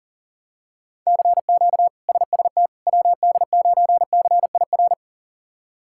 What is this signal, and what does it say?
Morse code, heard as a keyed tone, about 40 wpm, 700 Hz. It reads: CQ SST WD9CIR
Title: K1USN SST QSO Practice - 40wpm